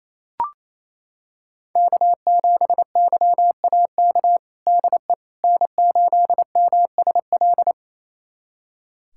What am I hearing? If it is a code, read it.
Morse code, heard as a keyed tone, about 27 wpm, 700 Hz. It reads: K7YAK DE N8MSL
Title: USA Callsign DE Callsign - 5 Characters - 28wpm